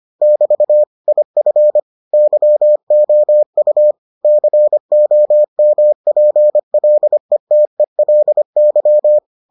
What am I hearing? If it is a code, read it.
Morse code, heard as a keyed tone, about 25 wpm, 600 Hz. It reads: <BT> IF YOU COMPLETELY